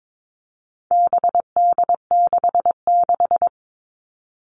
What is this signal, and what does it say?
Morse code, heard as a keyed tone, about 22 wpm, 700 Hz. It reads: BD66
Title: All Cut Numbers - 4-Digit 22wpm